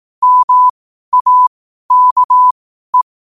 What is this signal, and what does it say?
Morse code, heard as a keyed tone, about 18 wpm, spaced as for 12 wpm, 1000 Hz. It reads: MAKE
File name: qotdfiles / 12wpm